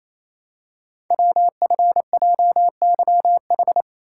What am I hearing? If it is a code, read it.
Morse code, heard as a keyed tone, about 28 wpm, 690 Hz. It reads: WFJYH
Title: Code Groups - 5 Letters - 28wpm